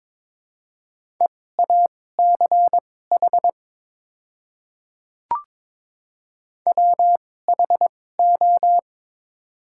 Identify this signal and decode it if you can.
Morse code, heard as a keyed tone, about 22 wpm, spaced as for 16 wpm, 700 Hz. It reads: EACH WHO
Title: 2x Character Spacing - Top 100 Words - 22wpm